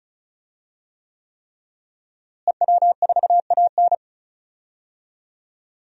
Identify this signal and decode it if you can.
Morse code, heard as a keyed tone, about 36 wpm, 700 Hz. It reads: EW4AN
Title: DXCC-100 Call Signs - 5 Characters - Repeated 3x - 35wpm